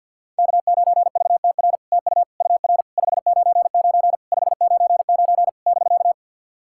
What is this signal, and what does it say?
Morse code, heard as a keyed tone, about 48 wpm, 700 Hz. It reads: K9VTR TU UR 599 599 <BK>
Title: SOTA QSO Practice 50wpm